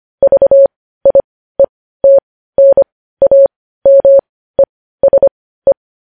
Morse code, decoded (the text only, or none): VIETNAMESE